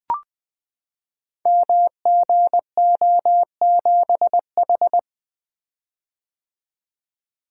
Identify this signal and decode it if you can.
Morse code, heard as a keyed tone, about 20 wpm, 700 Hz. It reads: MGO7H
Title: Code Groups - 5 Letter:Numbers - 20wpm